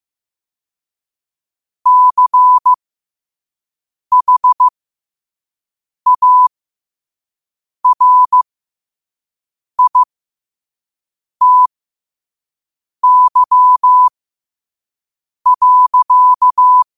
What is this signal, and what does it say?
Morse code, heard as a keyed tone, about 15 wpm, spaced as for 5 wpm, 1000 Hz. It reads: CHARITY.
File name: qotdfiles / 5wpm